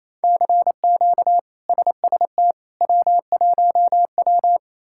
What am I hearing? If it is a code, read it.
Morse code, heard as a keyed tone, about 27 wpm, 700 Hz. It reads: CQ SST W1W
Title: K1USN SST QSO Practice - 28wpm